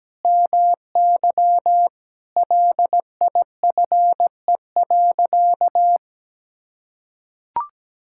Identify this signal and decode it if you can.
Morse code, heard as a keyed tone, about 17 wpm, 700 Hz. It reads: MY LIFE.
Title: Sentences from Top 1000 Words - Review Part 5 - 17wpm